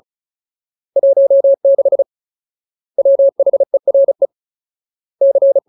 Morse code, heard as a keyed tone, about 34 wpm, 550 Hz. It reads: 16 WHERE C
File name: Ultimate Word List / 35WPM Ultimate